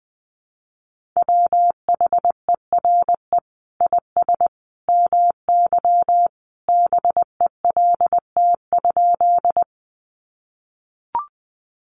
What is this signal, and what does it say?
Morse code, heard as a keyed tone, about 20 wpm, 700 Hz. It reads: WHERE IS MY BELT?
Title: Sentences from Top 1400 Words 20wpm